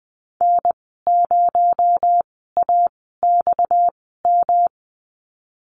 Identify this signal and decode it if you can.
Morse code, heard as a keyed tone, about 20 wpm, spaced as for 14 wpm, 700 Hz. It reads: N0AXM
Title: 245 Lesson 39g - Letter z - Call Signs 20wpm